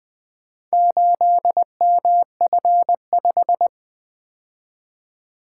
Morse code, heard as a keyed tone, about 20 wpm, 700 Hz. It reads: 8MF5
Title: Four Letter-Number Combinations - 20wpm